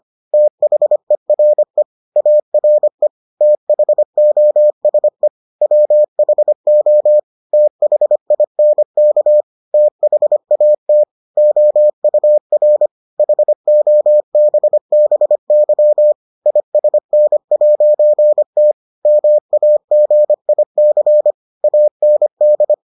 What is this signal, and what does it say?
Morse code, heard as a keyed tone, about 25 wpm, 600 Hz. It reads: THERE ARE THOSE WHO THINK THAT OUR HOBBY ISN'T MAGIC AND